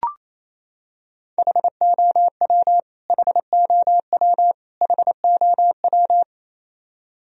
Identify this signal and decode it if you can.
Morse code, heard as a keyed tone, about 27 wpm, 700 Hz. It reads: HOW HOW HOW
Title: Top 100 Words Repeated 28wpm